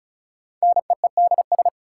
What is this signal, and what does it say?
Morse code, heard as a keyed tone, about 36 wpm, 700 Hz. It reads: NEEDS
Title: Sets of 3 Words - New Word Drill - 35wpm